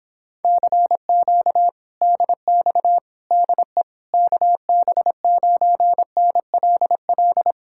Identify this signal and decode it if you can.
Morse code, heard as a keyed tone, about 26 wpm, 700 Hz. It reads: CQ DX DE KB9NLL